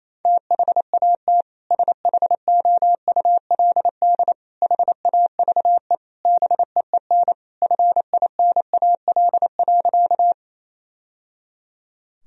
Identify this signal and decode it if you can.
Morse code, heard as a keyed tone, about 28 wpm, 700 Hz. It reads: THAT SHOULD HAVE BEEN FINAL.
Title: Sentences from Top 1000 Words - Review Part 2 - 28wpm